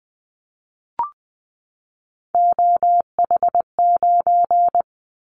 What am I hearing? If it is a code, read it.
Morse code, heard as a keyed tone, about 20 wpm, 700 Hz. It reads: OH9
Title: Callsign Prefixes for Top 25 DXCC - 20wpm